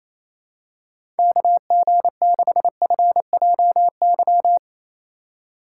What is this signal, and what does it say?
Morse code, heard as a keyed tone, about 28 wpm, 700 Hz. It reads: KG6FJY
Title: USA Call Signs - 6 Characters - 3x Character Spacing - 28wpm